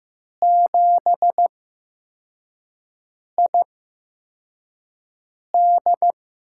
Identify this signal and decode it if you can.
Morse code, heard as a keyed tone, about 15 wpm, spaced as for 4 wpm, 700 Hz. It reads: 7ID
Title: USA Call Signs - 8x Character Spacing - 15wpm